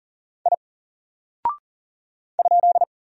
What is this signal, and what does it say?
Morse code, heard as a keyed tone, about 40 wpm, 700 Hz. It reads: I ?
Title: Single Character (Letter:Number:Punctuation:Prosign) 40wpm